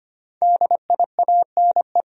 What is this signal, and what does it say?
Morse code, heard as a keyed tone, about 25 wpm, 700 Hz. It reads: DIANE
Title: USA Common Names - Max 5 Letters - 25wpm